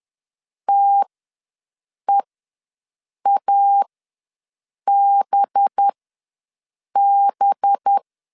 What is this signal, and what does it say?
Morse code, heard as a keyed tone, about 10 wpm, spaced as for 6 wpm, 800 Hz. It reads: TEABB
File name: LEZIONI / 5 LBVC / 5 LB ws 10wpm 3616